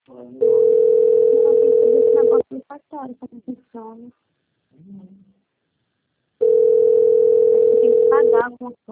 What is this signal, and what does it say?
Ringback tone (North America), 440 + 480 Hz, with speech.
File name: out- - - - - .45921